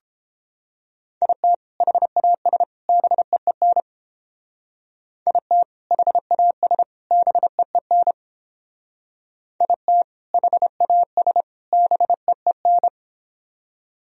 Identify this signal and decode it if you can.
Morse code, heard as a keyed tone, about 29 wpm, 700 Hz. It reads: IT HAS BEEN IT HAS BEEN IT HAS BEEN